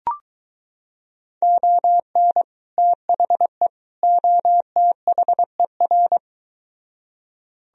Morse code, heard as a keyed tone, about 23 wpm, 700 Hz. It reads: ON THE OTHER